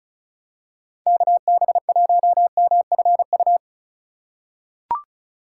Morse code, heard as a keyed tone, about 34 wpm, 700 Hz. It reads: KB1MFU
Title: USA Call Signs - 6 Characters - 2x Character Spacing - 35wpm